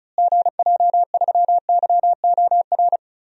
Morse code, heard as a keyed tone, about 36 wpm, 700 Hz. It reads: GJ3YOR
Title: DXCC-100 Call Signs - 6 Characters - Repeated 3x - 35wpm